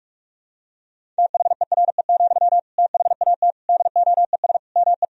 Morse code, heard as a keyed tone, about 42 wpm, 700 Hz. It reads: THERE, THAT DOES ME
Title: Sentences from Top 200 Words and Abbreviations 45wpm